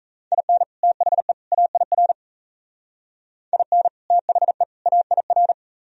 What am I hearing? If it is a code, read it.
Morse code, heard as a keyed tone, about 37 wpm, 700 Hz. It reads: IN THE AIR IN THE AIR